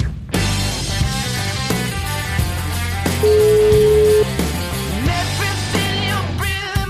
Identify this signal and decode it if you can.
Ringback tone (Europe, 425 Hz), with noise.